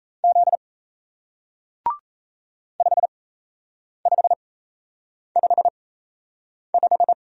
Morse code, heard as a keyed tone, about 37 wpm, spaced as for 8 wpm, 700 Hz. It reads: Z 5555